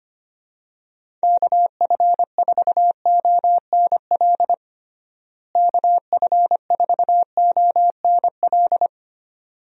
Morse code, heard as a keyed tone, about 25 wpm, 700 Hz. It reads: KF4ONL KF4ONL
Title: USA Call Signs - 6 Characters - Repeated 2x - 25wpm